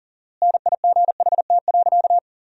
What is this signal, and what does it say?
Morse code, heard as a keyed tone, about 37 wpm, 700 Hz. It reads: NIGHT.
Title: 4x Word-Spacing - Sentences from Top 300 Words - 40wpm